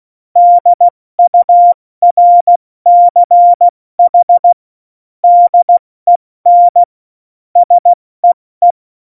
Morse code, heard as a keyed tone, about 16 wpm, 700 Hz. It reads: DURCH DEN SEE
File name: Morsekurs / Witze / Witz 08. Dezember 2020, 70 F80